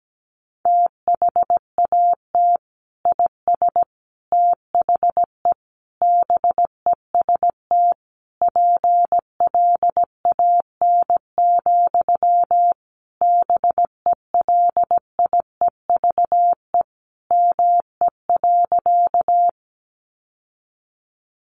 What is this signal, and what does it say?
Morse code, heard as a keyed tone, about 17 wpm, 700 Hz. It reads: THAT IS THE BEST PLAN, BELIEVE ME.
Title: Sentences from Top 1000 Words 17wpm